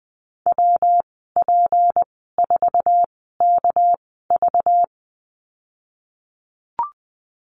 Morse code, heard as a keyed tone, about 20 wpm, spaced as for 14 wpm, 700 Hz. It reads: WP4KV
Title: 213 Lesson 34g - Number 8 - Call Signs 20wpm